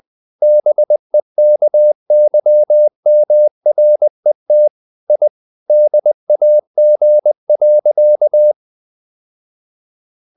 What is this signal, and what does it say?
Morse code, heard as a keyed tone, about 20 wpm, 600 Hz. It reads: BEKYMRET I DAG.
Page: Ferie 20 wpm